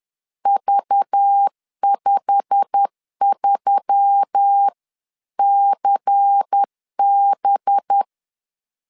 Morse code, heard as a keyed tone, about 10 wpm, 800 Hz. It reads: V53CB
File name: CW / DET / 1 9 1500C S A 10wpm 4511